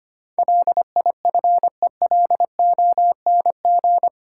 Morse code, heard as a keyed tone, about 25 wpm, 700 Hz. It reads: LIFELONG
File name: Eight Letter Words - 25wpm